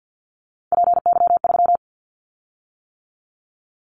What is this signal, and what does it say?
Morse code, heard as a keyed tone, about 50 wpm, 690 Hz. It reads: PY3
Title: Callsign Prefixes for Top 100 DXCC - 50wpm